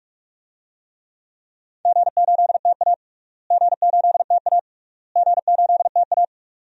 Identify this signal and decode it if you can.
Morse code, heard as a keyed tone, about 46 wpm, 700 Hz. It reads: G8TA G8TA G8TA
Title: DXCC-100 Call Signs - 4 Characters - Repeated 3x - 45wpm